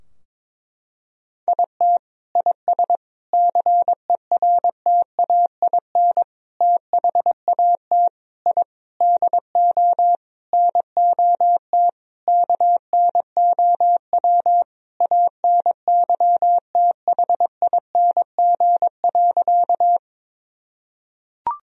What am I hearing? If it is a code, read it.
Morse code, heard as a keyed tone, about 22 wpm, 700 Hz. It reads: IT IS CERTAIN THAT I DO NOT KNOW ANYTHING.
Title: Sentences from Top 1000 Words - Review Part 2 - 22wpm